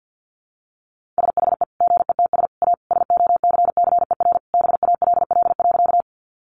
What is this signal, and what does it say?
Morse code, heard as a keyed tone, about 50 wpm, 700 Hz. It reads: SHE GETS A SOCCER BALL.
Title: Easy Phrases 50wpm